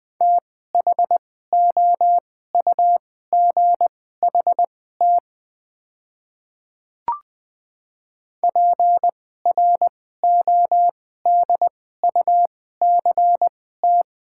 Morse code, heard as a keyed tone, about 20 wpm, spaced as for 14 wpm, 690 Hz. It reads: THOUGHT PRODUCT
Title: 2x Character Spacing - Top 500 Words - 20wpm